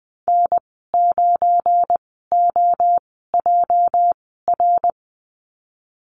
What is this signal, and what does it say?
Morse code, heard as a keyed tone, about 19 wpm, spaced as for 14 wpm, 700 Hz. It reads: N9OJR
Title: 219 Lesson 35g - Number 0 - Call Signs 20wpm